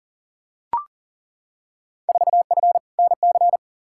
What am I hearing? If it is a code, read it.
Morse code, heard as a keyed tone, about 40 wpm, 700 Hz. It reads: 4F NC
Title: Field Day Exchanges - 40wpm